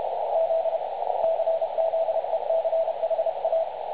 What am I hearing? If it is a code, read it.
Morse code, heard as a keyed tone, about 30 wpm, 650 Hz, with radio noise. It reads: Z CZECHA
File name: dlazaawansowanych / nagraniazpasma / websdr recording 3565.0kHz